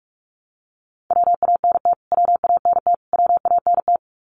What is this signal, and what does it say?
Morse code, heard as a keyed tone, about 44 wpm, 700 Hz. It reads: WANT WANT WANT
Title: Top 300 Words Repeated 45wpm